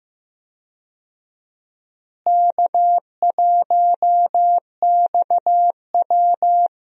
Morse code, heard as a keyed tone, about 15 wpm, 700 Hz. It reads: K1XW
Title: USA Call Signs - 4 Characters - Repeated 2x - 15wpm